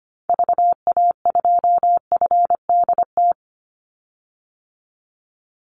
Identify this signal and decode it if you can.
Morse code, heard as a keyed tone, about 25 wpm, 700 Hz. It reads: VA2FDT
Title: Call Signs - 25wpm